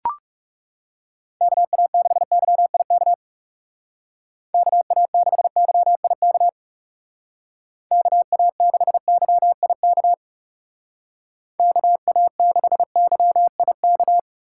Morse code, heard as a keyed tone, about 35 wpm, 700 Hz. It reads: KA6YIK KA6YIK KA6YIK KA6YIK
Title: Speed Racing - USA 2x3 Call Signs 30wpm